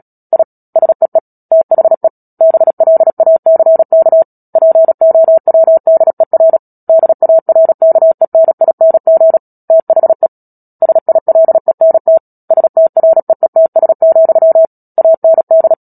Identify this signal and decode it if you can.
Morse code, heard as a keyed tone, about 37 wpm, 650 Hz. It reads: I SEE THE BLACK POWDER DARKENING THE SILENT STREETS, AND
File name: Book Two / 10-Chapter 10 - Epilogue - 36 WPM